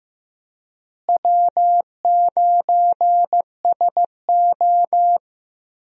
Morse code, heard as a keyed tone, about 15 wpm, 700 Hz. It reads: W9SO
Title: USA Call Signs - 4 Characters - Repeated 3x - 15wpm